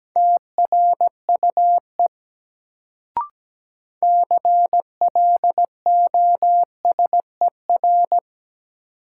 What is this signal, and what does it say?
Morse code, heard as a keyed tone, about 17 wpm, 700 Hz. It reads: TRUE CLOSER
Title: Sets of 5 Words Encore - New Word Drill - 17wpm